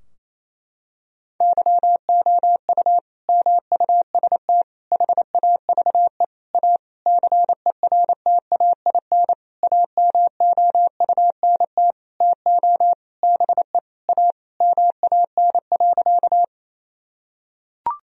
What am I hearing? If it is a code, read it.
Morse code, heard as a keyed tone, about 28 wpm, 700 Hz. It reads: YOU MUST HAVE A CERTAIN AMOUNT TO BE A MAN.
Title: Sentences from Top 1000 Words - Review Part 4 - 28wpm